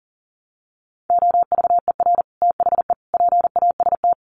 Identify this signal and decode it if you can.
Morse code, heard as a keyed tone, about 40 wpm, 700 Hz. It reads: OVER THE PAST
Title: Sets of 3 Words 40wpm